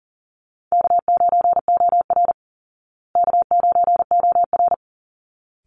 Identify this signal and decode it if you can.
Morse code, heard as a keyed tone, about 40 wpm, 700 Hz. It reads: K9OR K9OR
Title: Easy Call Signs Repeated 2x - 40wpm